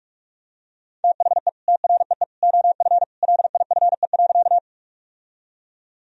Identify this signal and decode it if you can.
Morse code, heard as a keyed tone, about 41 wpm, 700 Hz. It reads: THE TREE OF LIFE.
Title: Sentences from Top 1000 Words - Review Part 1 - 45wpm